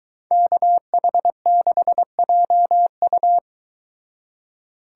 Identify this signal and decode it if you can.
Morse code, heard as a keyed tone, about 22 wpm, 700 Hz. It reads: KH6JU